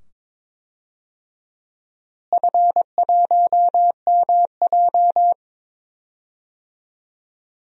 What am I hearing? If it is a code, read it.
Morse code, heard as a keyed tone, about 21 wpm, 700 Hz. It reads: F1MJ
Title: DXCC-100 Call Signs - 4 Characters - Repeated 2x - 22wpm